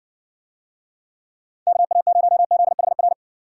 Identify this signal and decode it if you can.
Morse code, heard as a keyed tone, about 56 wpm, 700 Hz. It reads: KA0ZHR